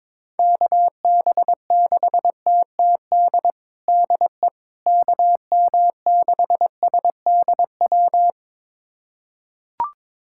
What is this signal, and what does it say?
Morse code, heard as a keyed tone, about 22 wpm, 700 Hz. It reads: KB6TTD DE KM6SDW
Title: USA Callsign DE Callsign - 6 Characters - 22wpm